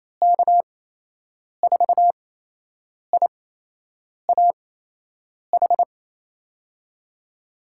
Morse code, heard as a keyed tone, about 28 wpm, spaced as for 8 wpm, 700 Hz. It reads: K4IAH
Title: USA Call Signs - 6 Characters - 8x Character Spacing - 28wpm